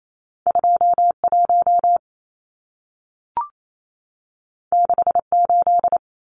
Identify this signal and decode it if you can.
Morse code, heard as a keyed tone, about 28 wpm, 700 Hz. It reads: 21 68
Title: Numbers - 2-Digit 28wpm